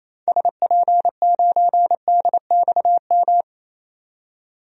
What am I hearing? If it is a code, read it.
Morse code, heard as a keyed tone, about 28 wpm, 700 Hz. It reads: SP9DXM
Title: DXCC-100 Call Signs - 6 Characters - 28wpm